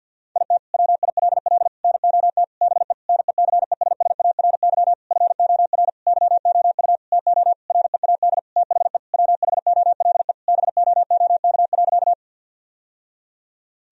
Morse code, heard as a keyed tone, about 52 wpm, 700 Hz. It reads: IT WILL NOT BE NECESSARY FOR YOU TO READ THE WHOLE BOOK.
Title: Sentences from Top 1000 Words - Review Part 1 - 50wpm